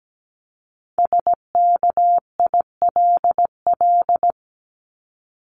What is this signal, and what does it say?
Morse code, heard as a keyed tone, about 17 wpm, 700 Hz. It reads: SKILL
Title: Top 1001-1250 Words 17wpm